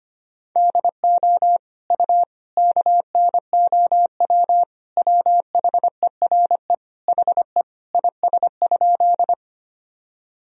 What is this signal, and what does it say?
Morse code, heard as a keyed tone, about 25 wpm, 700 Hz. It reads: DO U KNOW WHERE HE IS?
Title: Sentences from Top 300 Words and Abbreviations 25wpm